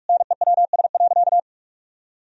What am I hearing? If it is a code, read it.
Morse code, heard as a keyed tone, about 46 wpm, 700 Hz. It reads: NEWS.
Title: Sentences from Top 1000 Words - Review Part 4 - 45wpm